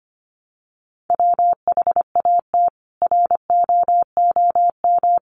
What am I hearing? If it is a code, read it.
Morse code, heard as a keyed tone, about 25 wpm, 700 Hz. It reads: WHAT ROOM
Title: Sentences from Top 600 Words 25wpm